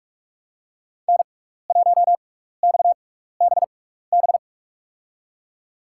Morse code, heard as a keyed tone, about 44 wpm, spaced as for 15 wpm, 690 Hz. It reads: N1XBB
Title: USA Call Signs - 5 Characters - 6x Character Spacing - 45wpm